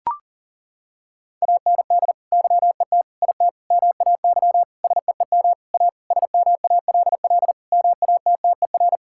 Morse code, heard as a keyed tone, about 40 wpm, 700 Hz. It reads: AND YET IT MAY SEEM A SMALL MATTER
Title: Sentences from Top 1000 Words - Review Part 5 - 40wpm